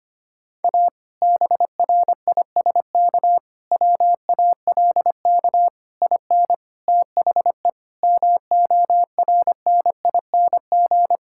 Morse code, heard as a keyed tone, about 25 wpm, 700 Hz. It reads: A BRISK WALK IN THE MORNING
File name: practice / 25wpm / Difficult Sentences with Hints - 25wpm